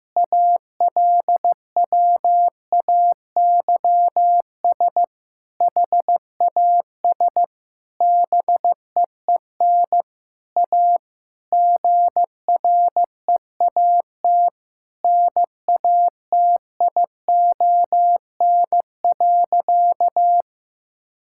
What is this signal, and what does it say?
Morse code, heard as a keyed tone, about 16 wpm, 700 Hz. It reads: ALWAYS HAS BEEN A GREAT NATION.
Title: Sentences from Top 1000 Words - Review Part 6 - 15wpm